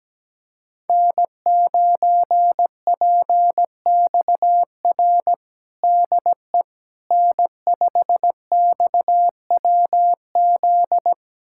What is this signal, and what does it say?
Morse code, heard as a keyed tone, about 17 wpm, 700 Hz. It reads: N9PXR DE N5XWZ
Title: USA Callsign DE Callsign - 5 Characters - 17wpm